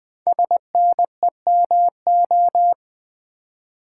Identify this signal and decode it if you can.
Morse code, heard as a keyed tone, about 20 wpm, 700 Hz. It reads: SNEMO
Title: Code Groups - 5 Letters - 20wpm